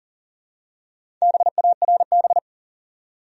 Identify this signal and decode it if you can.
Morse code, heard as a keyed tone, about 40 wpm, 700 Hz. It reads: BARB
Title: USA Common Names - Max 5 Letters - 40wpm